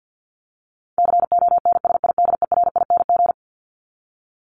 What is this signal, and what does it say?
Morse code, heard as a keyed tone, about 49 wpm, 700 Hz. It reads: CONSIDERING
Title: ING Suffix - Top 100 Words with Base Words 50wpm